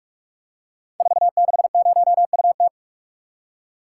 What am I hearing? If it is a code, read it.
Morse code, heard as a keyed tone, about 44 wpm, 690 Hz. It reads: 460UT
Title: Code Groups - 5 Letter:Numbers - 45wpm